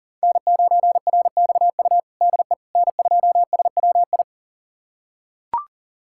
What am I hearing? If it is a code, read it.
Morse code, heard as a keyed tone, about 40 wpm, 700 Hz. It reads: N9RXU DE N2SWI
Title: USA Callsign DE Callsign - 5 Characters - 40wpm